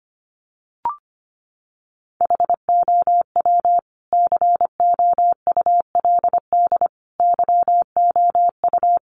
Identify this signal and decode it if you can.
Morse code, heard as a keyed tone, about 25 wpm, 700 Hz. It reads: HOW COULD YOU
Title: Sentences from Top 300 Words 25wpm